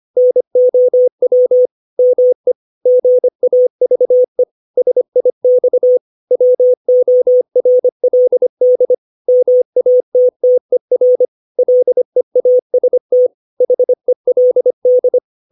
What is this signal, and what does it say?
Morse code, heard as a keyed tone, about 25 wpm, 500 Hz. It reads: NOW ME GAVE SIX WORLD MATTER LEAST HELD
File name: 632 words / 632 Words 25WPM